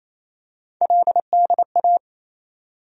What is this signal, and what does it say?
Morse code, heard as a keyed tone, about 28 wpm, 700 Hz. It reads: LDA
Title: USA Call Sign Postfixes - 3 Letters 28wpm